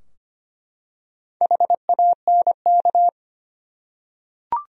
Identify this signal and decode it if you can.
Morse code, heard as a keyed tone, about 25 wpm, 700 Hz. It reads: HANK
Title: USA Common Names - Max 4 Letters - 25wpm